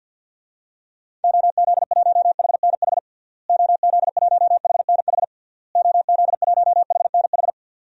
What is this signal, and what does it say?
Morse code, heard as a keyed tone, about 51 wpm, 700 Hz. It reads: OZ1HNH OZ1HNH OZ1HNH
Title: DXCC-100 Call Signs - 6 Characters - Repeated 3x - 50wpm